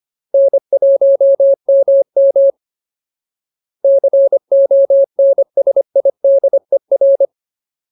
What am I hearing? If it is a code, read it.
Morse code, heard as a keyed tone, about 25 wpm, 550 Hz. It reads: N1MM CONSIDER
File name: Ultimate Word List / 25WPM Ultimate